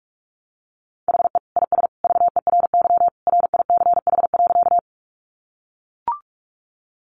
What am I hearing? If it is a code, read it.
Morse code, heard as a keyed tone, about 44 wpm, 700 Hz. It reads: HE IS VERY RICH.
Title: Sentences from Top 1100 Words 45wpm